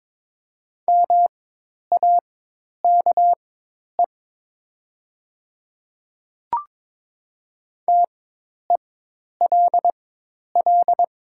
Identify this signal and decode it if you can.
Morse code, heard as a keyed tone, about 22 wpm, spaced as for 10 wpm, 700 Hz. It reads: MAKE TELL
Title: 4x Character Spacing - Top 500 Words - 22wpm